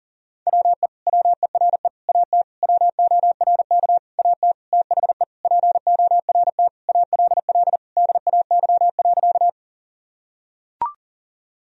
Morse code, heard as a keyed tone, about 37 wpm, 700 Hz. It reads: WE WERE AT WORK AT THE PORT ALL DAY.
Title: Sentences from Top 1000 Words - Review Part 1 - 40wpm